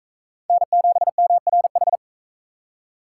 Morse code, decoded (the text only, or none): N7MRH